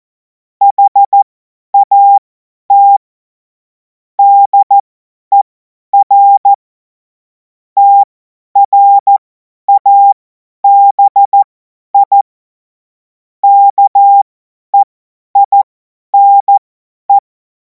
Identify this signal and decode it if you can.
Morse code, heard as a keyed tone, about 14 wpm, spaced as for 10 wpm, 800 Hz. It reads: HAT DER TRABI KEINE
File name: Morsekurs / Witze / Witz 16. Juni 2026 50 F70